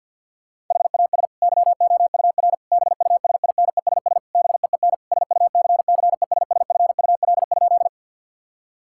Morse code, heard as a keyed tone, about 50 wpm, 700 Hz. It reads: HAS YOUR BUSINESS BEEN SUCCESSFUL?
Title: Sentences from Top 1000 Words - Review Part 6 - 50wpm